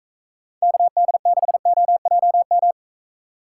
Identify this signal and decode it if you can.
Morse code, heard as a keyed tone, about 44 wpm, 700 Hz. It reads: KD6OJM